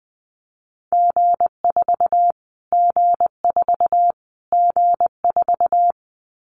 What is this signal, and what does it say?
Morse code, heard as a keyed tone, about 19 wpm, 700 Hz. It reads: G4 G4 G4
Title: Callsign Prefixes for Top 25 DXCC - Repeated 3x - 20wpm